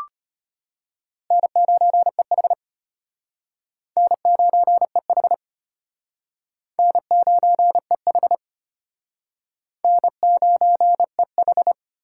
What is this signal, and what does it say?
Morse code, heard as a keyed tone, about 32 wpm, 700 Hz. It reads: N9EH N9EH N9EH N9EH